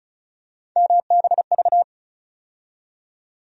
Morse code, heard as a keyed tone, about 36 wpm, 700 Hz. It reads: MBV
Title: USA Call Sign Postfixes - 3 Letters 35wpm